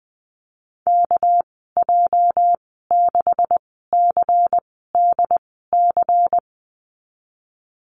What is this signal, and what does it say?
Morse code, heard as a keyed tone, about 20 wpm, spaced as for 14 wpm, 700 Hz. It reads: KJ6CDC
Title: 213 Lesson 34g - Number 8 - Call Signs 20wpm